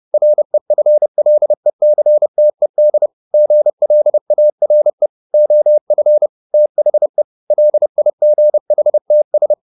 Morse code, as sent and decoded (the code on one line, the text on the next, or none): .-. ..-. .-.. -.-. - . -.. / --. .-.. .- .-. / --- ..-. / - .... / .-.. .. --. .... - ...
REFLECTED GLARE OF THE LIGHTS